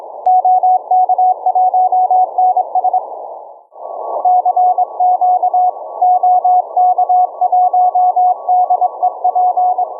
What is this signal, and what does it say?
Morse code, heard as a keyed tone, about 24 wpm, 720 Hz. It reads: OK1NS CQ OK1DEP